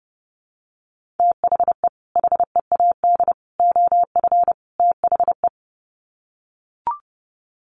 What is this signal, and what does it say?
Morse code, heard as a keyed tone, about 30 wpm, 700 Hz. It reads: THE HEAD OF THE
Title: Sets of 4 Words 30wpm